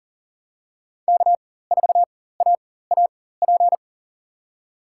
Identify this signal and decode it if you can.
Morse code, heard as a keyed tone, about 40 wpm, spaced as for 19 wpm, 700 Hz. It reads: K4AAP
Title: USA Call Signs - 5 Characters - 4x Character Spacing - 40wpm